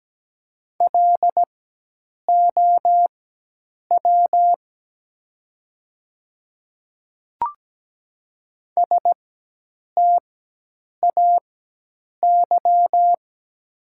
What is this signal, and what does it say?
Morse code, heard as a keyed tone, about 17 wpm, spaced as for 8 wpm, 700 Hz. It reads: LOW STAY
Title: 4x Character Spacing - Top 500 Words - 17wpm